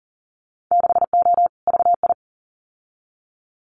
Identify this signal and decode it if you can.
Morse code, heard as a keyed tone, about 43 wpm, 700 Hz. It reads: 6O VI
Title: Winter Field Day Exchanges Encore 40wpm